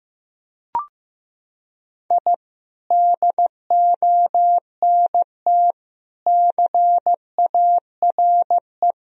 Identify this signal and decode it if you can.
Morse code, heard as a keyed tone, about 15 wpm, 700 Hz. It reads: I DONT CARE
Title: Sentences from Top 1000 Words - Review Part 4 - 15wpm